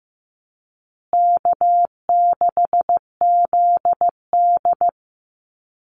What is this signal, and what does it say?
Morse code, heard as a keyed tone, about 15 wpm, 700 Hz. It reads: K6ZD
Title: Easy Call Signs Repeated 2x - 15wpm